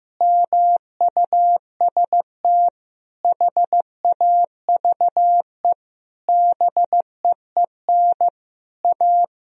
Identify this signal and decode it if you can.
Morse code, heard as a keyed tone, about 15 wpm, 700 Hz. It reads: MUST HAVE BEEN A
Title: Sentences from Top 1000 Words - Review Part 3 - 15wpm